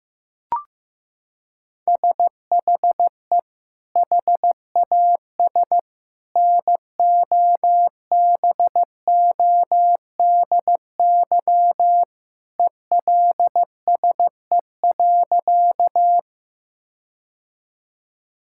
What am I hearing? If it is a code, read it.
Morse code, heard as a keyed tone, about 15 wpm, 700 Hz. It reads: SHE HAS NOBODY ELSE.
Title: Sentences from Top 1600 Words 15wpm